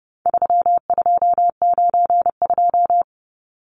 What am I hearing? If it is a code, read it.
Morse code, heard as a keyed tone, about 30 wpm, 700 Hz. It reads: 3292
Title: Numbers - 4-Digit 30wpm